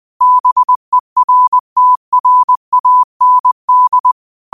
Morse code, heard as a keyed tone, about 20 wpm, 1000 Hz. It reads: BERTRAND
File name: qotdfiles / 20wpm